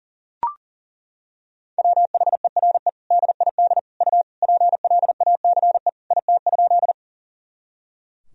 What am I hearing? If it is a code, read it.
Morse code, heard as a keyed tone, about 37 wpm, 700 Hz. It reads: WHERE DID U PLACE IT?
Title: Sentences from Top 200 Words and Abbreviations 40wpm